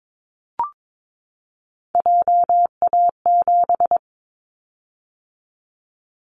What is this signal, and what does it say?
Morse code, heard as a keyed tone, about 22 wpm, 700 Hz. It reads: JA7
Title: Callsign Prefixes for Top 25 DXCC - 22wpm